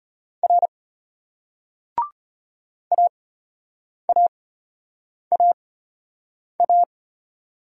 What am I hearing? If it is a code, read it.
Morse code, heard as a keyed tone, about 34 wpm, spaced as for 8 wpm, 700 Hz. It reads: R AAAA